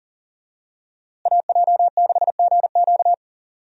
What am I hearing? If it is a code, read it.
Morse code, heard as a keyed tone, about 40 wpm, 700 Hz. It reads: AJ6GQ
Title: USA Call Signs - 5 Characters - Repeated 3x - 40wpm